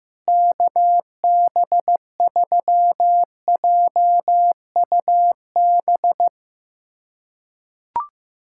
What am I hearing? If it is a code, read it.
Morse code, heard as a keyed tone, about 15 wpm, 700 Hz. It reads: KB3JUB
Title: USA Callsign DE Callsign - 6 Characters - 15wpm